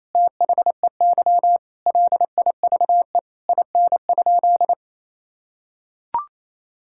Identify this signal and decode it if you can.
Morse code, heard as a keyed tone, about 28 wpm, 700 Hz. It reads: THEY LIVE IN?
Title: Sentences from Top 1000 Words - Review Part 5 - 28wpm